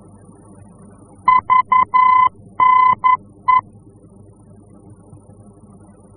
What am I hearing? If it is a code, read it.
Morse code, heard as a keyed tone, about 11 wpm, 1020 Hz. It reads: VNE
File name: france / mx / VNE-F-VOR